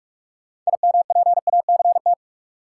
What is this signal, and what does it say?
Morse code, heard as a keyed tone, about 46 wpm, 700 Hz. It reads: IMPACT
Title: Sentences from Top 1000 Words - Review Part 6 - New Word Drill - 45wpm